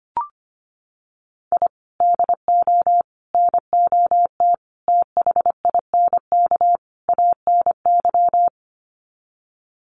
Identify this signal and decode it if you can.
Morse code, heard as a keyed tone, about 25 wpm, 700 Hz. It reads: I DO NOT THINK ANY
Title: Sets of 5 Words Encore 25wpm